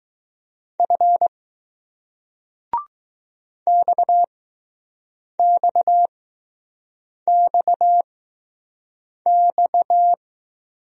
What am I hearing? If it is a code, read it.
Morse code, heard as a keyed tone, about 19 wpm, spaced as for 6 wpm, 700 Hz. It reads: F XXXX